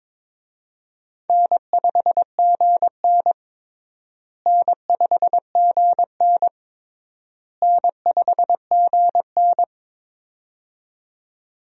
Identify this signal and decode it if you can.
Morse code, heard as a keyed tone, about 22 wpm, 700 Hz. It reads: N5GN N5GN N5GN
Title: USA Call Signs - 4 Characters - Repeated 3x - 22wpm